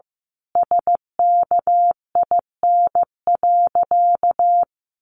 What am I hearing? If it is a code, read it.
Morse code, heard as a keyed tone, about 15 wpm, 700 Hz. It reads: SKIN.
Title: Sentences from Top 1300 Words 15wpm